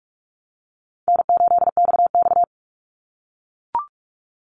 Morse code, heard as a keyed tone, about 44 wpm, 700 Hz. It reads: N8XX
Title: Easy Call Signs - 45wpm